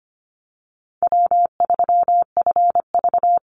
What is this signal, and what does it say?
Morse code, heard as a keyed tone, about 25 wpm, 700 Hz. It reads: W3FV
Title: Easy Call Signs - 25wpm